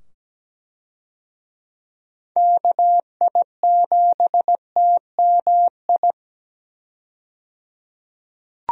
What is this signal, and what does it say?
Morse code, heard as a keyed tone, about 17 wpm, 700 Hz. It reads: KI7TMI
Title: USA Call Signs - 6 Characters - 17wpm